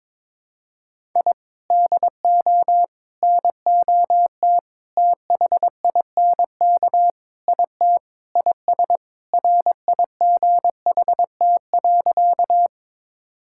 Morse code, heard as a keyed tone, about 22 wpm, 700 Hz. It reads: I DO NOT THINK IT IS RIGHT.
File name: practice / 22wpm / Sentences with 4-Word Phrases - 22wpm